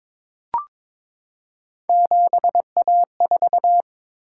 Morse code, heard as a keyed tone, about 22 wpm, 700 Hz. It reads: 7A4
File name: practice / 22wpm / Three Letter-Number Combinations - 22wpm